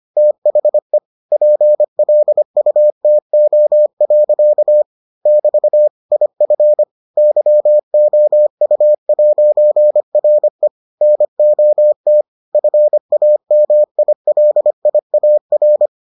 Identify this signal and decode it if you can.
Morse code, heard as a keyed tone, about 25 wpm, 600 Hz. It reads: THE PLUTO. <BT> IF YOU'RE NOT FAMILIAR